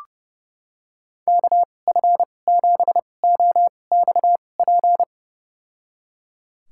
Morse code, heard as a keyed tone, about 32 wpm, spaced as for 22 wpm, 700 Hz. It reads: KF7OXP
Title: USA Call Signs - 6 Characters - 2x Character Spacing - 30wpm